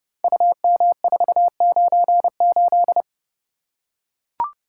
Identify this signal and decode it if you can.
Morse code, heard as a keyed tone, about 30 wpm, 700 Hz. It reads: UM498
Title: Code Groups - 5 Letter:Numbers - 30wpm